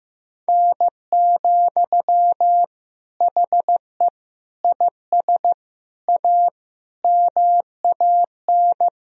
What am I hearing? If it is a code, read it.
Morse code, heard as a keyed tone, about 15 wpm, 700 Hz. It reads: N, HE IS A MAN
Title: Sentences from Top 400 Words and Abbreviations 15wpm